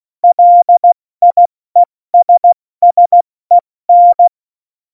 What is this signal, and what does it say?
Morse code, heard as a keyed tone, about 16 wpm, 700 Hz. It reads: LIESSEN
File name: Morsekurs / Witze / Witz 08. Januar 2019, 70 F80